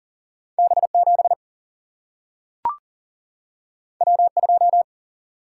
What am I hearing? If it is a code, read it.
Morse code, heard as a keyed tone, about 39 wpm, 700 Hz. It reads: B7 W2
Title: Single-Letter Single-Number 40wpm